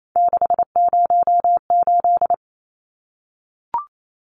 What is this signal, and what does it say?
Morse code, heard as a keyed tone, about 28 wpm, 700 Hz. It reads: 608
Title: Numbers - 3-Digit 28wpm